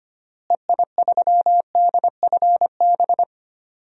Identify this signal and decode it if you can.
Morse code, heard as a keyed tone, about 25 wpm, 700 Hz. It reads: EI3DFB
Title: DXCC-100 Call Signs - 6 Characters - Repeated 3x - 25wpm